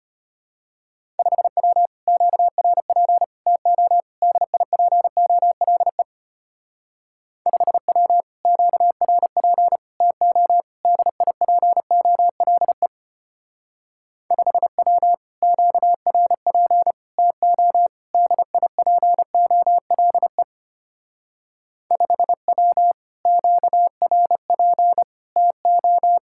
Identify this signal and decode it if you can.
Morse code, heard as a keyed tone, about 32 wpm, 700 Hz. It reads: 5W QRP TO DIPOLE 5W QRP TO DIPOLE 5W QRP TO DIPOLE 5W QRP TO